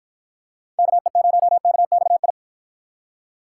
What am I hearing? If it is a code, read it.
Morse code, heard as a keyed tone, about 51 wpm, 700 Hz. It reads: KE0KXI